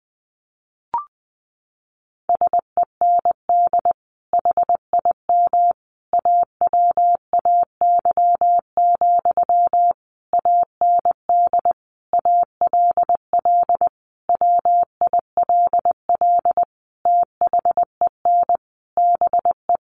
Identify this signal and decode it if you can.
Morse code, heard as a keyed tone, about 20 wpm, 700 Hz. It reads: SEND HIM AWAY, AND ALL WILL THEN BE
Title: Sentences from Top 1500 Words 20wpm